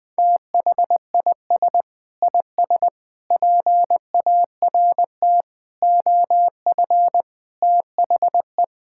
Morse code, heard as a keyed tone, about 20 wpm, 700 Hz. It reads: THIS IS PART OF THE
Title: Sentences with 2-Word Phrases - 20wpm